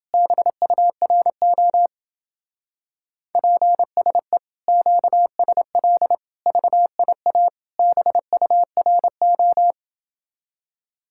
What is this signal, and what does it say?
Morse code, heard as a keyed tone, about 27 wpm, 700 Hz. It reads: BURO PSE QSL VIA BURO